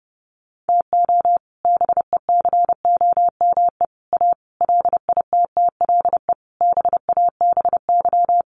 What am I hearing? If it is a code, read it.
Morse code, heard as a keyed tone, about 30 wpm, 700 Hz. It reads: TO BECOME A LITTLE BABY
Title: Sentences from Top 1300 Words 30wpm